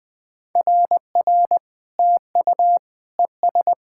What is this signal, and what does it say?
Morse code, heard as a keyed tone, about 20 wpm, 690 Hz. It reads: RR TU ES
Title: SOTA QSO Practice 20wpm